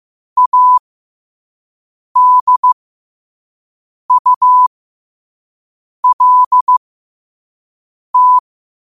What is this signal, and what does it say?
Morse code, heard as a keyed tone, about 15 wpm, spaced as for 5 wpm, 1000 Hz. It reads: ADULT
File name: 5wpm